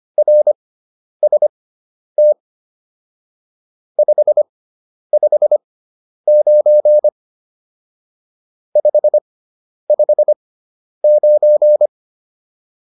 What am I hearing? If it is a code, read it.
Morse code, heard as a keyed tone, about 25 wpm, spaced as for 10 wpm, 600 Hz. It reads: RST 559 559